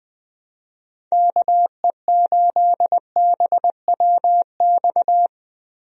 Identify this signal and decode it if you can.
Morse code, heard as a keyed tone, about 21 wpm, 700 Hz. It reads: KE8BWX
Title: USA Call Signs - 6 Characters - Repeated 3x - 20wpm